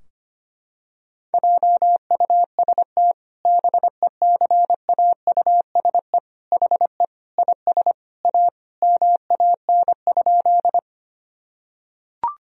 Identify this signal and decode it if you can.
Morse code, heard as a keyed tone, about 25 wpm, 700 Hz. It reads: JUST BECAUSE HE IS A MAN?
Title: Sentences from Top 1000 Words - Review Part 3 - 25wpm